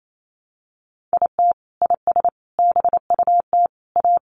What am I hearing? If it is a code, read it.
Morse code, heard as a keyed tone, about 28 wpm, 700 Hz. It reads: IT IS BUT A
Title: Sentences from Top 1000 Words 28wpm